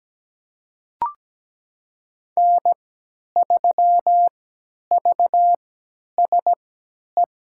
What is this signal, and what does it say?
Morse code, heard as a keyed tone, about 17 wpm, spaced as for 10 wpm, 700 Hz. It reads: N3VSE
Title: USA Call Signs - 5 Characters - 3x Character Spacing - 17wpm